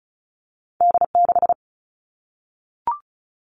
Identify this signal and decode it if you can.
Morse code, heard as a keyed tone, about 36 wpm, 700 Hz. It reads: D6
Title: All Cut Numbers - 2-Digit 35wpm